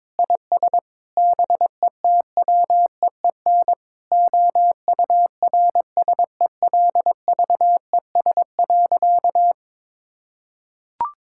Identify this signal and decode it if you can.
Morse code, heard as a keyed tone, about 22 wpm, 700 Hz. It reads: IS BETWEEN OURSELVES.
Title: Sentences from Top 1600 Words 22wpm